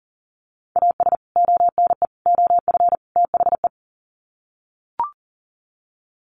Morse code, heard as a keyed tone, about 40 wpm, 700 Hz. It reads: AS ONE OF THE
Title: Sets of 4 Words 40wpm